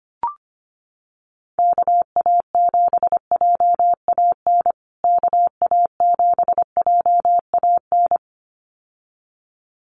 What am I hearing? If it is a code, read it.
Morse code, heard as a keyed tone, about 25 wpm, 700 Hz. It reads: KA7JAN KA7JAN
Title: USA Call Signs Repeated 2x 25wpm